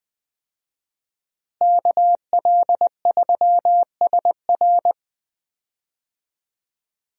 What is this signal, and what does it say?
Morse code, heard as a keyed tone, about 20 wpm, 700 Hz. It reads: KL3SR
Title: USA Call Signs - 5 Characters - Repeated 3x - 20wpm